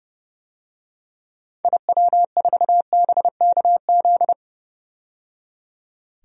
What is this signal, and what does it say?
Morse code, heard as a keyed tone, about 31 wpm, 700 Hz. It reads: IW4BKZ
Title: DXCC-100 Call Signs - 6 Characters - 30wpm